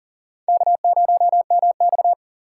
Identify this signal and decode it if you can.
Morse code, heard as a keyed tone, about 40 wpm, 700 Hz. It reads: K0MX
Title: USA Call Signs - 4x Character Spacing - 40wpm